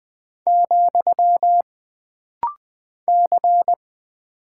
Morse code, heard as a keyed tone, about 20 wpm, 700 Hz. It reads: , C
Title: Single Character (Letter:Number:Punctuation:Prosign) - Rapid-Fire - 20wpm